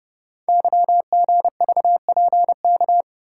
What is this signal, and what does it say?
Morse code, heard as a keyed tone, about 30 wpm, 700 Hz. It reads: YGVPK
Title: Code Groups - 5 Letters - 30wpm